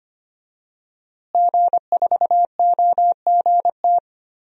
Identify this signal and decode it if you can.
Morse code, heard as a keyed tone, about 25 wpm, 700 Hz. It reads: G4OGT
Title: DXCC-100 Call Signs - 5 Characters - Repeated 2x - 25wpm